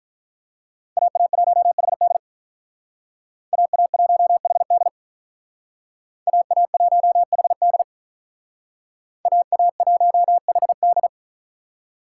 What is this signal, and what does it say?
Morse code, heard as a keyed tone, about 43 wpm, 700 Hz. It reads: AA1HD AA1HD AA1HD AA1HD
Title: Speed Racing - USA 2x2 Call Signs 35wpm